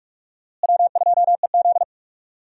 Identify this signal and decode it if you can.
Morse code, heard as a keyed tone, about 46 wpm, 700 Hz. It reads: W2EZ
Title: USA Call Signs - 4 Characters - Repeated 3x - 45wpm